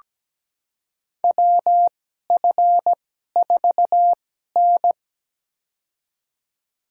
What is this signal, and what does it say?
Morse code, heard as a keyed tone, about 17 wpm, spaced as for 12 wpm, 700 Hz. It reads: WF4N
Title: USA Call Signs - 4 Characters - 2x Character Spacing - 17wpm